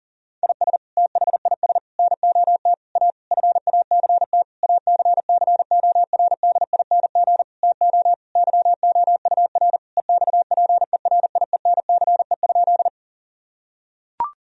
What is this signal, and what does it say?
Morse code, heard as a keyed tone, about 40 wpm, 700 Hz. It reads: IS THIS NOT A FACT ACCORDING TO YOUR EXPERIENCE?
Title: Sentences from Top 1000 Words - Review Part 4 - 40wpm